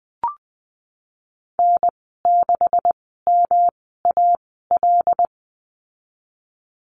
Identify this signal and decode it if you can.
Morse code, heard as a keyed tone, about 20 wpm, spaced as for 15 wpm, 700 Hz. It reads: N6MAL
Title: 186 Lesson 30g - Letter b - Call Signs 20wpm